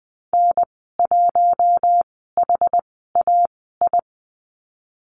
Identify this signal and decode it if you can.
Morse code, heard as a keyed tone, about 20 wpm, spaced as for 14 wpm, 700 Hz. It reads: N1HAI
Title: 069 Lesson 12g - Letter d - Call Signs 20wpm